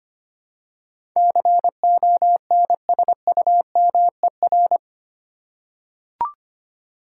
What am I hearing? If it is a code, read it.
Morse code, heard as a keyed tone, about 25 wpm, 700 Hz. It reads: CONSUMER
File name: 8 Letter Words - First 3 Letters Given - 25wpm